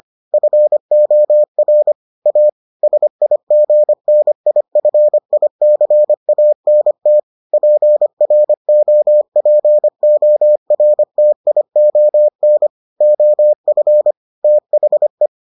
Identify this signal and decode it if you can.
Morse code, heard as a keyed tone, about 25 wpm, 600 Hz. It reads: FOR A SIGNIFICANT PROPORTION OF THE